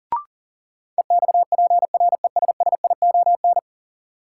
Morse code, heard as a keyed tone, about 40 wpm, 700 Hz. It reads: EXPRESSION
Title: Sentences from Top 1200 Words - New Word Drill - 40wpm